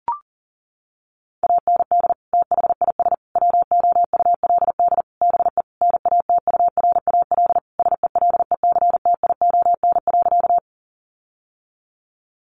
Morse code, heard as a keyed tone, about 40 wpm, 700 Hz. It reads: AND THIS WOULD BE NATURAL SELECTION.
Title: Sentences from Top 1400 Words 40wpm